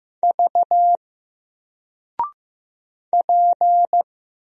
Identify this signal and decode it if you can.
Morse code, heard as a keyed tone, about 15 wpm, 700 Hz. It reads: V P
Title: Single Letter-Number 15wpm